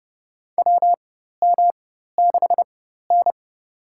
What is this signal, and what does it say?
Morse code, heard as a keyed tone, about 32 wpm, spaced as for 14 wpm, 700 Hz. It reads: WM6N
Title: USA Call Signs - 4 Characters - 4x Character Spacing - 30wpm